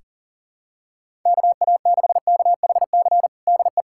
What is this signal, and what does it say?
Morse code, heard as a keyed tone, about 40 wpm, 700 Hz. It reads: KA6KHC DE
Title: USA Callsign DE Callsign - 6 Characters - 40wpm